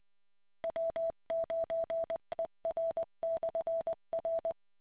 Morse code, heard as a keyed tone, about 20 wpm, 660 Hz. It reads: W9ER/R